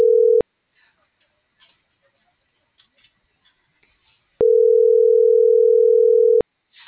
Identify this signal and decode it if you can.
Ringback tone (North America), 440 + 480 Hz.